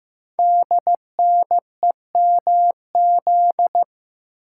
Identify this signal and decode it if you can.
Morse code, heard as a keyed tone, about 15 wpm, 700 Hz. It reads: DNEMZ
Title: Code Groups - 5 Letters - 15wpm